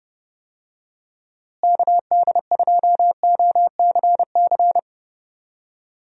Morse code, heard as a keyed tone, about 30 wpm, 700 Hz. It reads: KD2OCC
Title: USA Call Signs - 6 Characters - Repeated 2x - 30wpm